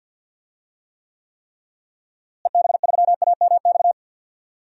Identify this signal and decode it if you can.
Morse code, heard as a keyed tone, about 52 wpm, 700 Hz. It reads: EB3AMX
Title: DXCC-100 Call Signs - 6 Characters - Repeated 3x - 50wpm